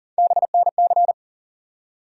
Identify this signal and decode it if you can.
Morse code, heard as a keyed tone, about 40 wpm, 700 Hz. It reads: BNC
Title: USA Call Sign Postfixes - 3 Letters 40wpm